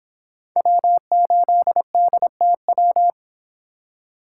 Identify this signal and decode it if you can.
Morse code, heard as a keyed tone, about 24 wpm, 700 Hz. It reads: W8DTW